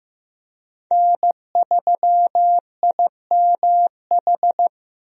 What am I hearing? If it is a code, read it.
Morse code, heard as a keyed tone, about 15 wpm, 700 Hz. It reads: N3IMH
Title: USA Call Signs - 5 Characters - Repeated 2x - 15wpm